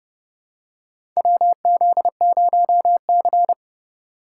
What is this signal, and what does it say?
Morse code, heard as a keyed tone, about 32 wpm, 700 Hz. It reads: WZ0C
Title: USA Call Signs - 4 Characters - 2x Character Spacing - 30wpm